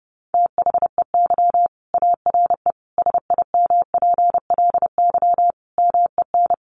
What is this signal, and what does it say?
Morse code, heard as a keyed tone, about 30 wpm, 700 Hz. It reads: THEY ARE SIMPLY MEN
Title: Sentences from Top 1500 Words 30wpm